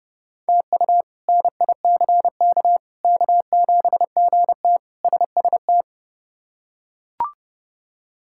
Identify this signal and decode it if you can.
Morse code, heard as a keyed tone, about 32 wpm, 700 Hz. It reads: TU NICK K7GT SST
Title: K1USN SST QSO Practice - 30wpm